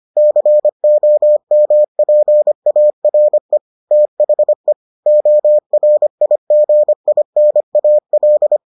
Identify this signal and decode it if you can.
Morse code, heard as a keyed tone, about 25 wpm, 600 Hz. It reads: COMPARE THE ORIGINAL